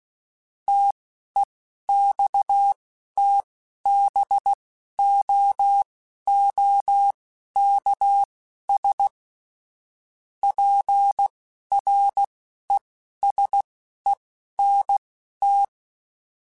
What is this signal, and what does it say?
Morse code, heard as a keyed tone, about 16 wpm, spaced as for 11 wpm, 800 Hz. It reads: TEXTBOOKS PRESENT
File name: Technical Subjects / Technical Subjects-mp3s / Gates Intro @10